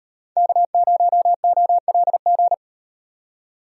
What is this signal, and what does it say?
Morse code, heard as a keyed tone, about 38 wpm, 700 Hz. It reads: K0OLG